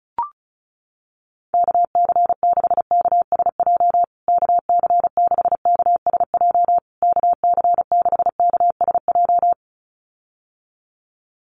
Morse code, heard as a keyed tone, about 34 wpm, 700 Hz. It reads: KC6KSJ KC6KSJ KC6KSJ
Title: USA Call Signs Repeated 3x 35wpm